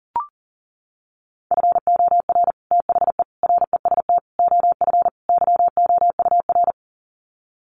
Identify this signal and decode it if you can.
Morse code, heard as a keyed tone, about 40 wpm, 700 Hz. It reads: FOR THE REST OF YOUR
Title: Sets of 5 Words Encore 40wpm